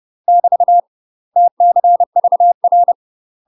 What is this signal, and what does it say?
Morse code, heard as a keyed tone, about 30 wpm, 700 Hz. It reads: <BT> TCVR
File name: UK MASTER QSOs TRAINING / 30WPM / UK MASTER QSOs 0027